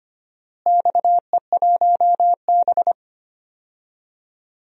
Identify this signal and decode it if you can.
Morse code, heard as a keyed tone, about 25 wpm, 700 Hz. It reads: XE1B
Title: DXCC-100 Call Signs - 4 Characters - Repeated 2x - 25wpm